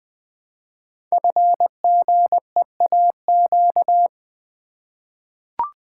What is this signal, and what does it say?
Morse code, heard as a keyed tone, about 20 wpm, 700 Hz. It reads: FGEAQ
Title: Code Groups - 5 Letters - 20wpm